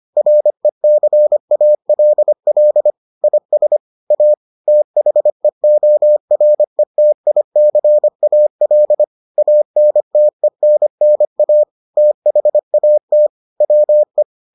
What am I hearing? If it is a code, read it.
Morse code, heard as a keyed tone, about 25 wpm, 600 Hz. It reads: RECALL IS A THEORETICAL ANTENNA THAT WE